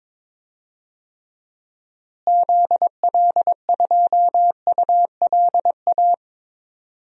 Morse code, heard as a keyed tone, about 21 wpm, 700 Hz. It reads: ZL2ULA
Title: DXCC-100 Call Signs - 6 Characters - Repeated 2x - 22wpm